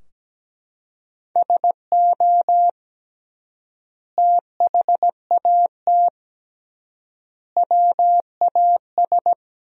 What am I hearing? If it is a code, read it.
Morse code, heard as a keyed tone, about 17 wpm, 700 Hz. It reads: SO THAT WAS
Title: 3x Word-Spacing - Sentences from Top 100 Words - 17wpm